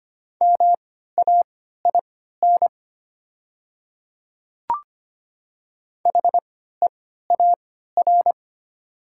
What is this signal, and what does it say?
Morse code, heard as a keyed tone, about 25 wpm, spaced as for 14 wpm, 700 Hz. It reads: MAIN HEAR
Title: 3x Character Spacing - Top 300 Words - 25wpm